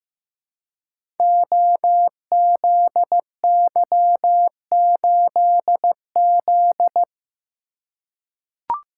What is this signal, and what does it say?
Morse code, heard as a keyed tone, about 15 wpm, 700 Hz. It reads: OZY8Z
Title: Code Groups - 5 Letter:Numbers - 15wpm